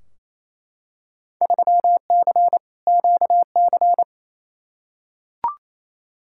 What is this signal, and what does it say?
Morse code, heard as a keyed tone, about 28 wpm, 700 Hz. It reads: 3C QC
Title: Field Day Exchanges Encore - 28wpm